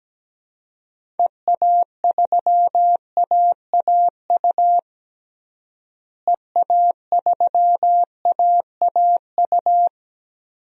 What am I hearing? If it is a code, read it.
Morse code, heard as a keyed tone, about 17 wpm, 700 Hz. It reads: EA3AAU EA3AAU
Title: DXCC-100 Call Signs - 6 Characters - Repeated 3x - 17wpm